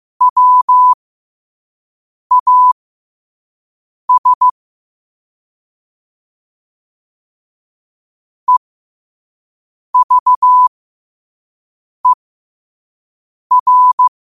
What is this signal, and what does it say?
Morse code, heard as a keyed tone, about 15 wpm, spaced as for 5 wpm, 1000 Hz. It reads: WAS EVER
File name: qotdfiles / 5wpm